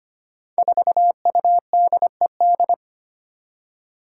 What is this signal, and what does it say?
Morse code, heard as a keyed tone, about 25 wpm, 700 Hz. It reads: 4UDED
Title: Five Challenging Letter-Number Combinations 25wpm